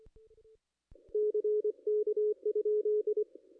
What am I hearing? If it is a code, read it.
Morse code, heard as a keyed tone, about 24 wpm, 420 Hz. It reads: CK?
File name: contests / ss-cw-k5tr-op-n5rz -2200